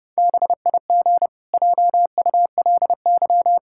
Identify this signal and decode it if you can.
Morse code, heard as a keyed tone, about 30 wpm, 700 Hz. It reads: BIG JULY
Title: Pangram Challenge v2 - 30wpm